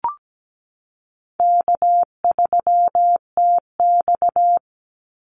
Morse code, heard as a keyed tone, about 17 wpm, 700 Hz. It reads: K3TX
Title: Easy Call Signs Repeated 2x - 17wpm